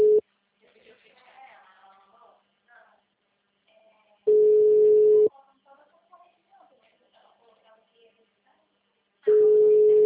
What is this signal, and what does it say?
Ringback tone (Europe, 425 Hz), with speech.